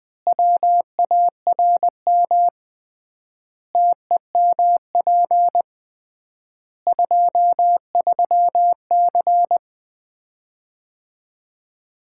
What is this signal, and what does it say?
Morse code, heard as a keyed tone, about 20 wpm, 700 Hz. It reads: WARM TEMP 23C
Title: Common QSO Phrases - 3x Word-Spacing - 20wpm